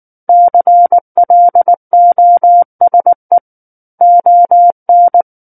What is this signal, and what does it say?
Morse code, heard as a keyed tone, about 19 wpm, 700 Hz. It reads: CLOSE ON
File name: Book One / 10-Chapter 10 - 19 WPM